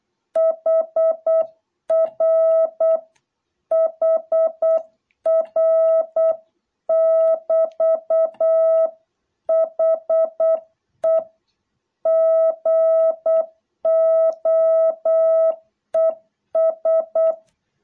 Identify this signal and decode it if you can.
Morse code, heard as a keyed tone, about 8 wpm, 660 Hz. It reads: HR HR<BT>HE GOES